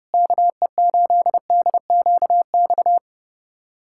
Morse code, heard as a keyed tone, about 32 wpm, 700 Hz. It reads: KE8DQX
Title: USA Callsign DE Callsign 30wpm